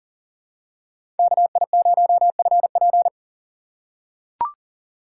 Morse code, heard as a keyed tone, about 40 wpm, 700 Hz. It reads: KI0FP
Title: USA Call Signs - 5 Characters - 3x Character Spacing - 40wpm